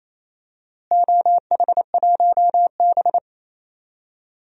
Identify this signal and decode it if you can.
Morse code, heard as a keyed tone, about 28 wpm, 700 Hz. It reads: OH1B
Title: DXCC-100 Call Signs - 4 Characters - Repeated 3x - 28wpm